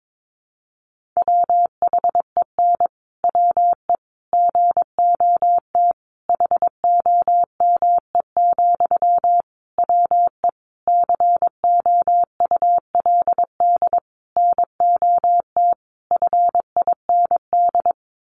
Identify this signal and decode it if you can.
Morse code, heard as a keyed tone, about 22 wpm, 700 Hz. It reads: WHEN WE GOT HOME, WE COULD NOT FIND
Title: Sentences from Top 1400 Words 22wpm